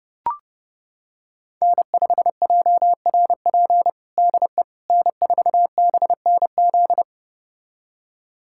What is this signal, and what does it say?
Morse code, heard as a keyed tone, about 32 wpm, 700 Hz. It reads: N5JRP DE N4BNZ
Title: USA Callsign DE Callsign - 5 Characters - 30wpm